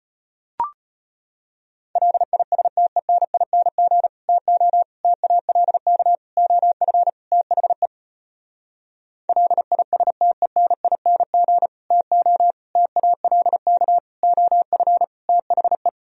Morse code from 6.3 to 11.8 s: OF THE LISTENING

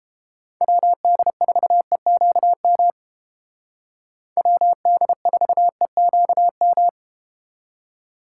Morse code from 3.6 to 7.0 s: WD4EQM

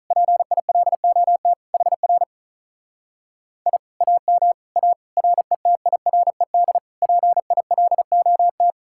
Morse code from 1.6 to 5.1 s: HR I AM A

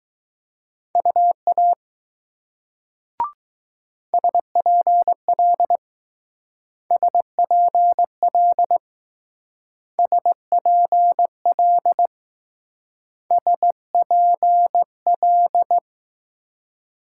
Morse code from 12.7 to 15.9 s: SPL

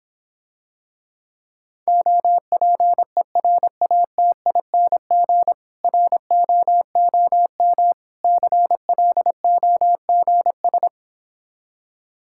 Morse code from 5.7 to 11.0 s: ROOM CLOGS

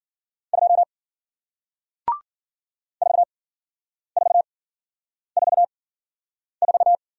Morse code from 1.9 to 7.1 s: 4444

